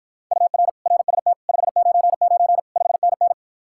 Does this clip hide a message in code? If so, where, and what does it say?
Yes: Morse code from 0.3 to 3.3 s, heard as a keyed tone, about 53 wpm, 700 Hz. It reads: UR RST 599 5NN